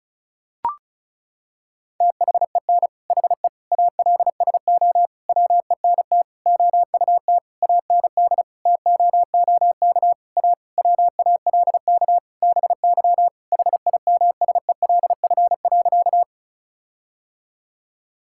Morse code, heard as a keyed tone, about 36 wpm, 700 Hz. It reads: THEN HE ALSO WENT OUT AND TOOK A WALK BY HIMSELF.
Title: Sentences from Top 1600 Words 35wpm